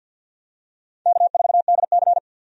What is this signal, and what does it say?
Morse code, heard as a keyed tone, about 51 wpm, 700 Hz. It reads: K4DC
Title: USA Call Signs - 4 Characters - 50wpm